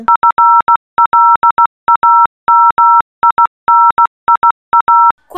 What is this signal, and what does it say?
Morse code, heard as a keyed tone, about 16 wpm, 1340 Hz. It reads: FLAMINIA
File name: telegrafo-5a-disney